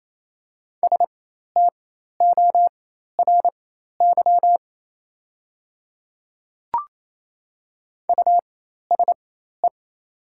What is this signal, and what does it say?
Morse code, heard as a keyed tone, about 28 wpm, spaced as for 13 wpm, 700 Hz. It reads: STORY USE
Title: 4x Character Spacing - Top 300 Words - 28wpm